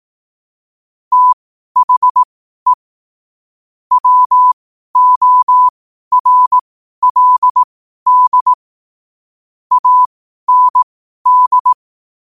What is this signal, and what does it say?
Morse code, heard as a keyed tone, about 18 wpm, spaced as for 12 wpm, 1000 Hz. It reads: THE WORLD AND
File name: qotdfiles / 12wpm